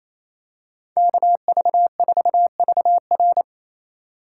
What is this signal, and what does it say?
Morse code, heard as a keyed tone, about 28 wpm, 700 Hz. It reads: KV4VR
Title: USA Call Signs - 4x Character Spacing - 28wpm